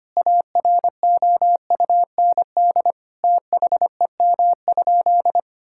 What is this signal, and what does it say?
Morse code, heard as a keyed tone, about 25 wpm, 700 Hz. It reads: AROUND THEM?
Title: Sentences from Top 1000 Words - Review Part 5 - 25wpm